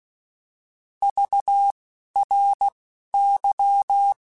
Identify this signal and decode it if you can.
Morse code, heard as a keyed tone, about 16 wpm, spaced as for 11 wpm, 800 Hz. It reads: VRY